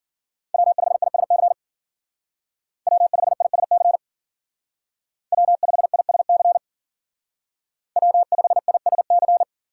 Morse code, heard as a keyed tone, about 48 wpm, 700 Hz. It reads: W5ISC W5ISC W5ISC W5ISC
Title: Speed Racing - USA 1x3 Call Signs 40wpm